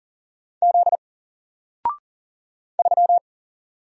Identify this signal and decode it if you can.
Morse code, heard as a keyed tone, about 38 wpm, 700 Hz. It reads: Z 3
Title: Single Letter-Number Alt 40wpm